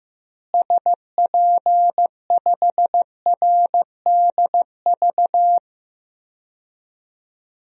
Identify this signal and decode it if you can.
Morse code, heard as a keyed tone, about 15 wpm, 700 Hz. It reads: SP5RDV
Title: DXCC-100 Call Signs - 6 Characters - Repeated 3x - 15wpm